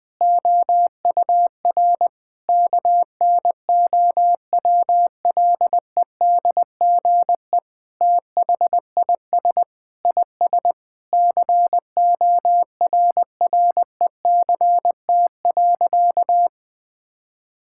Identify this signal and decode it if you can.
Morse code, heard as a keyed tone, about 20 wpm, 700 Hz. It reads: OUR KNOWLEDGE THIS IS CORRECT.
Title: Sentences with 8-Word Phrases - 20wpm